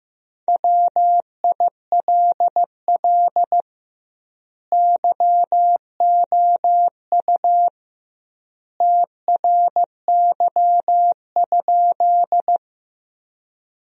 Morse code, heard as a keyed tone, about 15 wpm, 700 Hz. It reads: WILL YOU TRY?
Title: 2x Word-Spacing - Sentences from Top 200 Words - 15wpm